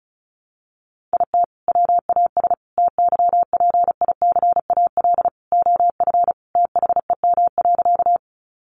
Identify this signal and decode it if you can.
Morse code, heard as a keyed tone, about 34 wpm, 700 Hz. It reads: IT WAS TYPICAL OF THEM.
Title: Sentences from Top 1100 Words 35wpm